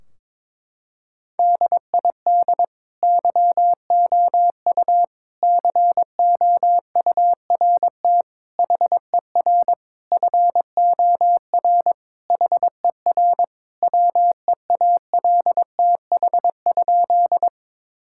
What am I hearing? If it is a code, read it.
Morse code, heard as a keyed tone, about 21 wpm, 700 Hz. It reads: DID YOU COURT HER FOR HER WEALTH?
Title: Sentences from Top 1600 Words 22wpm